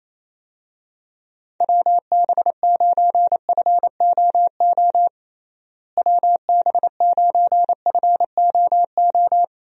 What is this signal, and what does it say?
Morse code, heard as a keyed tone, about 28 wpm, 700 Hz. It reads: WB9FOO WB9FOO
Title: USA Call Signs - 6 Characters - Repeated 3x - 28wpm